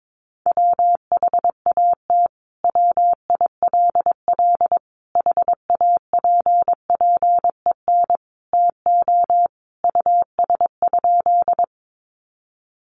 Morse code, heard as a keyed tone, about 22 wpm, 700 Hz. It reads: WHAT WILL HAPPEN TO US?
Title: Sentences from Top 400 Words 22wpm